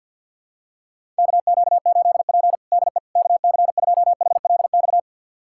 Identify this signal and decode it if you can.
Morse code, heard as a keyed tone, about 48 wpm, 700 Hz. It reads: KQ8P DE KC2HLX
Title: USA Callsign DE Callsign 50wpm